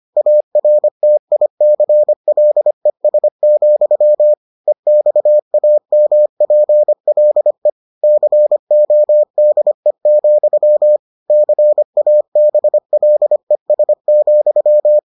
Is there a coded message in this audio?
Yes: Morse code from 0.2 to 15.0 s, heard as a keyed tone, about 25 wpm, 600 Hz. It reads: ARTICLES, EXAMPLE CODE, CABLES,